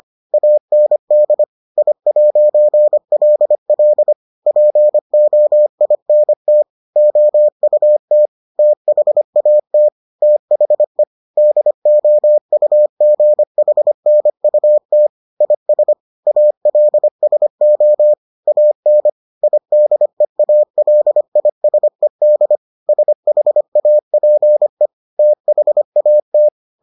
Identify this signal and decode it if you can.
Morse code, heard as a keyed tone, about 25 wpm, 600 Hz. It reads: AND I'LL POINT OUT THAT THE DOUGHNUT IS ALSO AN IDEALISED SHAPE THAT